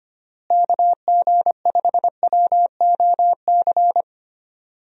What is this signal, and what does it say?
Morse code, heard as a keyed tone, about 25 wpm, 700 Hz. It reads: KG5WOC
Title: USA Call Signs - 6 Characters - Repeated 3x - 25wpm